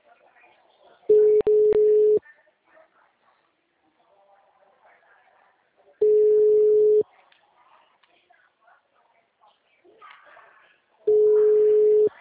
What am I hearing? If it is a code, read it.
Ringback tone (Europe, 425 Hz), with speech.
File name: out- -2006